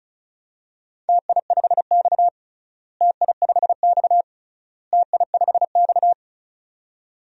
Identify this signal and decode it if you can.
Morse code, heard as a keyed tone, about 36 wpm, 700 Hz. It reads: TI5X TI5X TI5X
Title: DXCC-100 Call Signs - 4 Characters - Repeated 3x - 35wpm